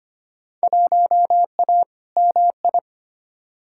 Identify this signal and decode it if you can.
Morse code, heard as a keyed tone, about 25 wpm, 700 Hz. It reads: 1A MI
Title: Field Day Exchanges - 25wpm